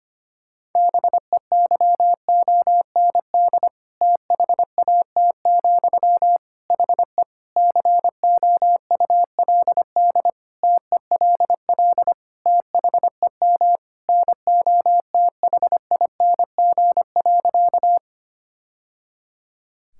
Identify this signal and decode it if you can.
Morse code, heard as a keyed tone, about 25 wpm, 700 Hz. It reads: BEYOND THAT, HE COULD TELL THEM NOTHING.
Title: Sentences from Top 1000 Words - Review Part 6 - 25wpm